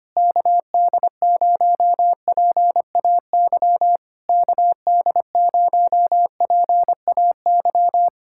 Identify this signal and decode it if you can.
Morse code, heard as a keyed tone, about 24 wpm, 700 Hz. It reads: KD0PAY KD0PAY
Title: Speed-Racing Common QSO Phrases - 25wpm